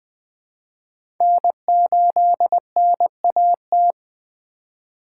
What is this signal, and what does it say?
Morse code, heard as a keyed tone, about 20 wpm, 700 Hz. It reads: N8NAT
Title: USA Call Signs - 5 Characters - 2x Character Spacing - 20wpm